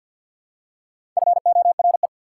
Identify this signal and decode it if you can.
Morse code, heard as a keyed tone, about 54 wpm, 690 Hz. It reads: FORE
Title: Prefixes and Suffixes 50wpm